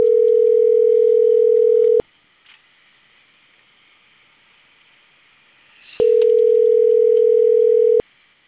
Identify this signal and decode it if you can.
Ringback tone (North America), 440 + 480 Hz.